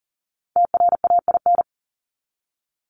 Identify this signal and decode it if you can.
Morse code, heard as a keyed tone, about 40 wpm, 700 Hz. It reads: TRAIN
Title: Top 1751-2000 Words 40wpm